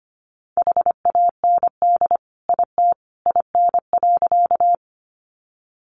Morse code, heard as a keyed tone, about 25 wpm, 700 Hz. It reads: HAND IT IN.
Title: Sentences from Top 200 Words 25wpm